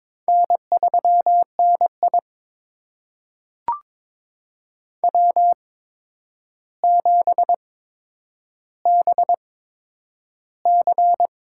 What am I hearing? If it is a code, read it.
Morse code, heard as a keyed tone, about 21 wpm, spaced as for 9 wpm, 700 Hz. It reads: N3NI W7BC
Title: USA Call Signs - 4 Characters - 8x Character Spacing - 22wpm